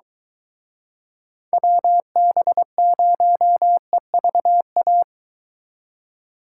Morse code, heard as a keyed tone, about 23 wpm, 700 Hz. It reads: WB0EVA